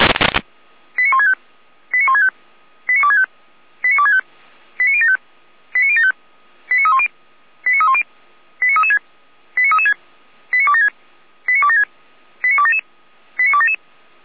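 Five-tone selective call (ZVEI), heard as quick runs of five tones, tones of about 80 ms. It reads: 89165 89165 89265 89265 89075 89075 89310 89310 89336 89336 89276 89276 89277 89277
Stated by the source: ZVEI five-tone selective call received on SDR.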